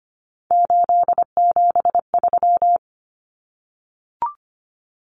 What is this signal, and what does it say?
Morse code, heard as a keyed tone, about 25 wpm, 700 Hz. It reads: 873
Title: Common Cut Numbers - 3-Digit 25wpm